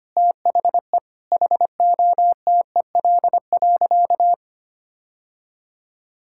Morse code, heard as a keyed tone, about 25 wpm, 700 Hz. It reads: THE HOTEL.
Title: Sentences from Top 1600 Words 25wpm